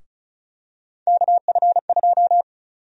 Morse code, heard as a keyed tone, about 34 wpm, 700 Hz. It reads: KF2
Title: USA Call Sign Prefixes - 35wpm